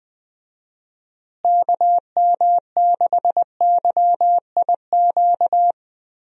Morse code, heard as a keyed tone, about 20 wpm, 700 Hz. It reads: KM6YIQ
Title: USA Call Signs - 6 Characters - Repeated 2x - 20wpm